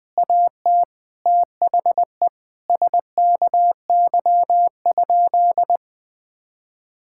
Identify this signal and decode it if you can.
Morse code, heard as a keyed tone, about 20 wpm, 700 Hz. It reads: AT THE SKY?
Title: Sentences with 4-Word Phrases - 20wpm